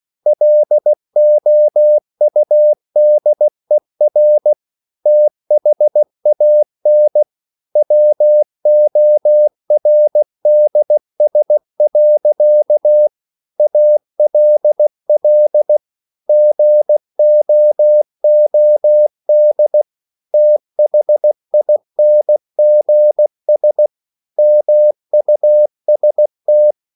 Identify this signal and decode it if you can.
Morse code, heard as a keyed tone, about 16 wpm, 600 Hz. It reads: LOUDER THAN WORDS. ALL GOOD THINGS MUST
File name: CW / Practice / EnglishSayings16